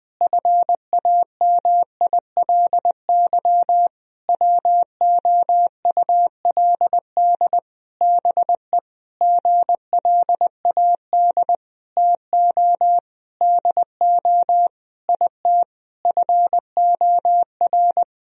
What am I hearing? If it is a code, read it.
Morse code, heard as a keyed tone, about 20 wpm, 700 Hz. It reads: FAMILY WOULD BE GLAD TO DO IT FOR
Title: Sentences from Top 1000 Words - Review Part 3 - 20wpm